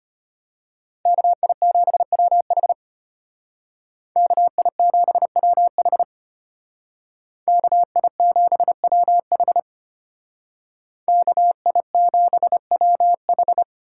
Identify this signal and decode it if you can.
Morse code, heard as a keyed tone, about 32 wpm, 700 Hz. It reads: KI7WH KI7WH KI7WH KI7WH